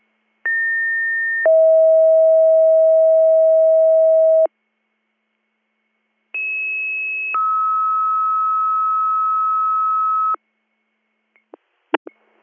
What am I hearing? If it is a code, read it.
Two-tone page, heard as a one-second tone, then a three-second tone, about 1800 then 640 Hz, about 2600 then 1290 Hz. It is heard from 0.4 to 10.4 s.